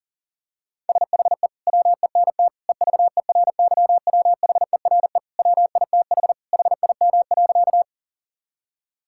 Morse code, heard as a keyed tone, about 37 wpm, 700 Hz. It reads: SHE WENT EVERYWHERE WITH HIM.
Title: Sentences from Top 1600 Words 40wpm